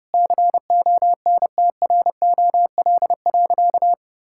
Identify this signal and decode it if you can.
Morse code, heard as a keyed tone, about 30 wpm, 700 Hz. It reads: CONTROL.
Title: Sentences from Top 1000 Words - Review Part 5 - 30wpm